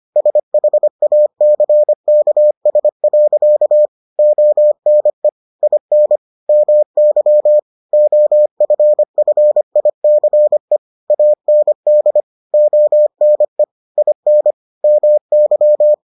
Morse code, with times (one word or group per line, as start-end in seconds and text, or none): 0.2-3.9 SHACKS.
4.2-5.3 ONE
5.6-6.2 IN
6.5-7.6 MY
7.9-10.8 OFFICE
11.1-12.2 AND
12.5-13.6 ONE
14.0-14.5 IN
14.8-15.9 MY